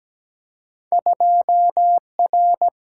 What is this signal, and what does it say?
Morse code, heard as a keyed tone, about 17 wpm, 700 Hz. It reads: 2R
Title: Single-Letter Single-Number 17wpm